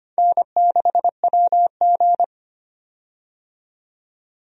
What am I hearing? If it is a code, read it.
Morse code, heard as a keyed tone, about 25 wpm, 700 Hz. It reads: N6WG
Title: USA Call Signs - 4 Characters - 25wpm